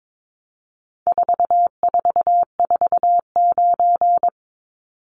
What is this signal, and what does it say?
Morse code, heard as a keyed tone, about 22 wpm, 700 Hz. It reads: 4449
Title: Numbers - 4-Digit 22wpm